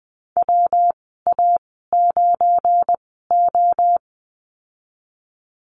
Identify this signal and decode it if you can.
Morse code, heard as a keyed tone, about 20 wpm, spaced as for 15 wpm, 700 Hz. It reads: WA9O
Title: 179 Lesson 29g - Slash - Call Signs 20wpm